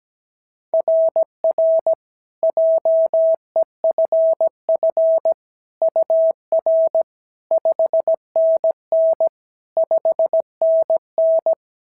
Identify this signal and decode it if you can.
Morse code, heard as a keyed tone, about 17 wpm, 690 Hz. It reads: RR JEFF UR 5NN 5NN
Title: SOTA QSO Practice 17wpm